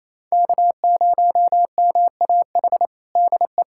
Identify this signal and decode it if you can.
Morse code, heard as a keyed tone, about 27 wpm, 700 Hz. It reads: K0MAH DE
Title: USA Callsign DE Callsign 28wpm